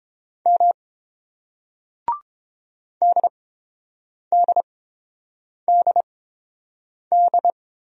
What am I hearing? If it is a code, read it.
Morse code, heard as a keyed tone, about 30 wpm, spaced as for 7 wpm, 700 Hz. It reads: M DDDD